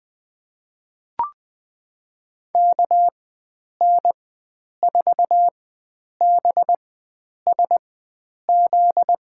Morse code, heard as a keyed tone, about 20 wpm, spaced as for 9 wpm, 700 Hz. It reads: KN4BSZ
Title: USA Call Signs - 6 Characters - 4x Character Spacing - 20wpm